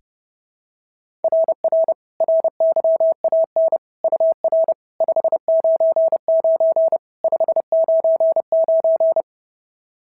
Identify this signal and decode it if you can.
Morse code, heard as a keyed tone, about 30 wpm, 690 Hz. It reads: RR RYAN UR 599 599
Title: SOTA QSO Practice 30wpm